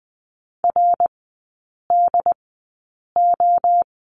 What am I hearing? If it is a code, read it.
Morse code, heard as a keyed tone, about 20 wpm, spaced as for 8 wpm, 700 Hz. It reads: RDO
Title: 166 Lesson 27e - Number 7 - Three Characters 20wpm